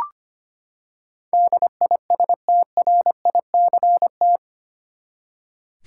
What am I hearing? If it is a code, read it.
Morse code, heard as a keyed tone, about 25 wpm, 700 Hz. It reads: DISTRICT
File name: Eight Letter Words - 25wpm